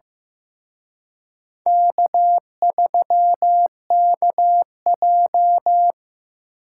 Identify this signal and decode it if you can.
Morse code, heard as a keyed tone, about 18 wpm, 700 Hz. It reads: K3KJ
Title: Speed Racing - USA 1x2 Call Signs 15wpm